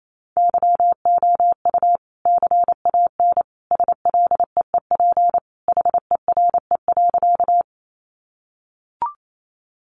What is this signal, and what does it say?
Morse code, heard as a keyed tone, about 28 wpm, 700 Hz. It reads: YOU CAN SLEEP HERE.
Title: Sentences from Top 500 Words 28wpm